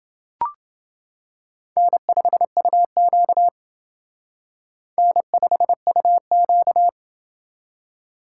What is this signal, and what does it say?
Morse code, heard as a keyed tone, about 28 wpm, 700 Hz. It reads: N5UQ N5UQ